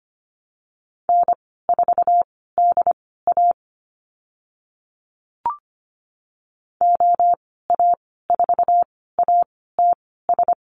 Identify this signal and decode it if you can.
Morse code, heard as a keyed tone, about 25 wpm, spaced as for 16 wpm, 700 Hz. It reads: N4DA OA4ATS
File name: Lesson 15g - Number 5 - Call Signs 25wpm